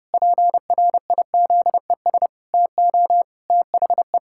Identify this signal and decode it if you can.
Morse code, heard as a keyed tone, about 30 wpm, 700 Hz. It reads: PRIZES TO THE
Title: Pangram Challenge v2 - 30wpm